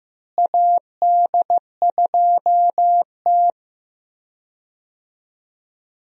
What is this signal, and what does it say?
Morse code, heard as a keyed tone, about 15 wpm, 700 Hz. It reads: AD2T
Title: USA Call Signs - 4 Characters - Repeated 3x - 15wpm